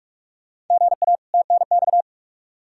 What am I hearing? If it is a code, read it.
Morse code, heard as a keyed tone, about 43 wpm, 700 Hz. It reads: GA TNX
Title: K1USN SST QSO Practice 45wpm